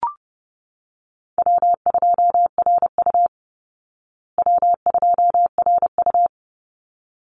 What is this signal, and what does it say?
Morse code, heard as a keyed tone, about 30 wpm, 700 Hz. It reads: W2RU W2RU
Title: Easy Call Signs Repeated 2x - 30wpm